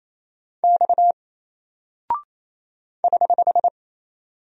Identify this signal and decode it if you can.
Morse code, heard as a keyed tone, about 27 wpm, 700 Hz. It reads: X <HH>
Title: Single Character (Letter:Number:Punctuation:Prosign) 28wpm